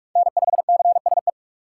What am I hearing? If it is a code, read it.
Morse code, heard as a keyed tone, about 44 wpm, 700 Hz. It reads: N5CSE
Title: USA Call Signs - 5 Characters - 6x Character Spacing - 45wpm